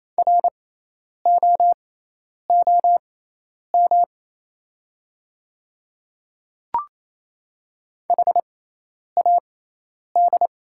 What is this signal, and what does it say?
Morse code, heard as a keyed tone, about 27 wpm, spaced as for 10 wpm, 700 Hz. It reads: ROOM HAD
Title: 6x Character Spacing - Top 300 Words - 28wpm